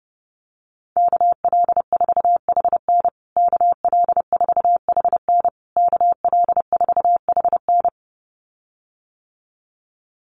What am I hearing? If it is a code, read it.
Morse code, heard as a keyed tone, about 30 wpm, 700 Hz. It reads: KL4HN KL4HN KL4HN
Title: USA Call Signs Repeated 3x 30wpm